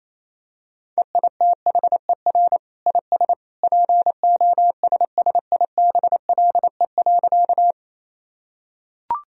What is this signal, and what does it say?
Morse code, heard as a keyed tone, about 28 wpm, 700 Hz. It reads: EITHER IS POSSIBLE.
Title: Sentences from Top 1000 Words - Review Part 4 - 28wpm